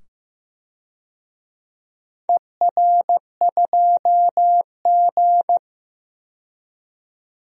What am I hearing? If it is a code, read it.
Morse code, heard as a keyed tone, about 16 wpm, 700 Hz. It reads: ER2G
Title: DXCC-100 Call Signs - 4 Characters - 15wpm